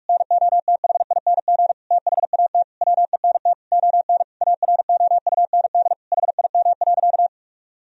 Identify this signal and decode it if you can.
Morse code, heard as a keyed tone, about 41 wpm, 700 Hz. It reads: NOTHING THAT WENT ON AROUND HIM.
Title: Sentences from Top 1000 Words - Review Part 3 - 45wpm